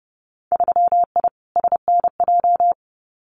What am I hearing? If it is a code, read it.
Morse code, heard as a keyed tone, about 30 wpm, 700 Hz. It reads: 3I SNJ
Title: Winter Field Day Exchanges Encore 30wpm